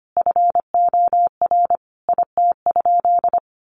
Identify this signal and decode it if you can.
Morse code, heard as a keyed tone, about 25 wpm, 700 Hz. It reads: FOR IT?
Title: Sentences from Top 1300 Words 25wpm